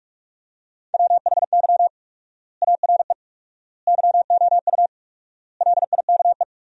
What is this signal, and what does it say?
Morse code, heard as a keyed tone, about 41 wpm, 700 Hz. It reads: WHY ARE YOU LIKE
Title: 4x Word-Spacing - Sentences from Top 200 Words - 45wpm